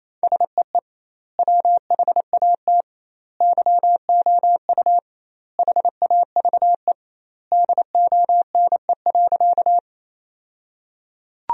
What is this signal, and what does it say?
Morse code, heard as a keyed tone, about 28 wpm, 700 Hz. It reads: SEE WHAT YOU HAVE DONE.
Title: 2x Word-Spacing - Sentences from Top 500 Words - 28wpm